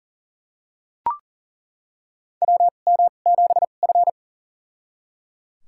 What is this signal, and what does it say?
Morse code, heard as a keyed tone, about 37 wpm, spaced as for 28 wpm, 700 Hz. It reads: WM7F
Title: USA Call Signs - 2x Character Spacing - 40wpm